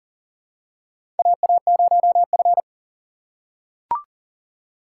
Morse code, heard as a keyed tone, about 40 wpm, 690 Hz. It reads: AA0F
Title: USA Call Signs - 4 Characters - 2x Character Spacing - 40wpm